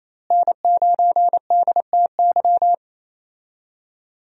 Morse code, heard as a keyed tone, about 28 wpm, 700 Hz. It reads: N9DTY
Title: USA Call Signs - 5 Characters - Repeated 3x - 28wpm